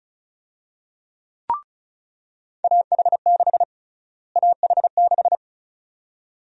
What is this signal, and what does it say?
Morse code, heard as a keyed tone, about 36 wpm, 700 Hz. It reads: AH6 AH6
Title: USA Call Sign Prefixes - Repeated 2x - 35wpm